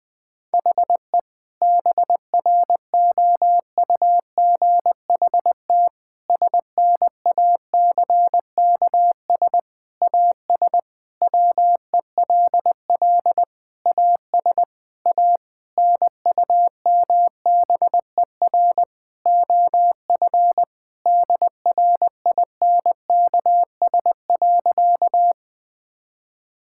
Morse code, heard as a keyed tone, about 21 wpm, 700 Hz. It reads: HE BROUGHT SNACKS AS WELL AS A NUMBER OF DRINKS.
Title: Sentences with 6-Word Phrases - 20wpm